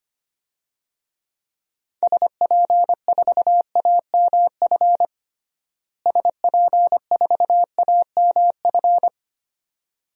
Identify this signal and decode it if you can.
Morse code, heard as a keyed tone, about 25 wpm, 700 Hz. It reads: SP4AMF SP4AMF
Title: DXCC-100 Call Signs - 6 Characters - Repeated 2x - 25wpm